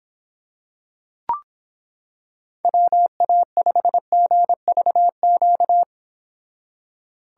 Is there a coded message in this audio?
Yes: Morse code from 2.6 to 5.8 s, heard as a keyed tone, about 25 wpm, 700 Hz. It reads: WA5GVQ